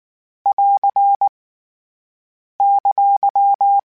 Morse code, heard as a keyed tone, about 19 wpm, 800 Hz. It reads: <AR> !
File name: Abreviations / Abreviations mp3s / Abreviations@18